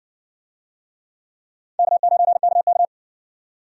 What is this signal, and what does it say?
Morse code, heard as a keyed tone, about 60 wpm, 700 Hz. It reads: K9KK